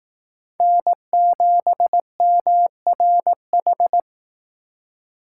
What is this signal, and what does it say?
Morse code, heard as a keyed tone, about 18 wpm, 700 Hz. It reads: N7MRH
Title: Speed Racing - USA 1x3 Call Signs 15wpm